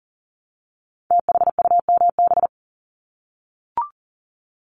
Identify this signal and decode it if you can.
Morse code, heard as a keyed tone, about 39 wpm, 700 Hz. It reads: THUMB
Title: Multi-letter Phonemes Repeated 3x 40wpm